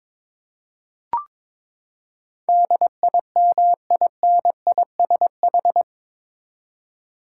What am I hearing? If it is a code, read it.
Morse code, heard as a keyed tone, about 22 wpm, 700 Hz. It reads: DIMINISH
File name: practice / 22wpm / Eight Letter Words - 22wpm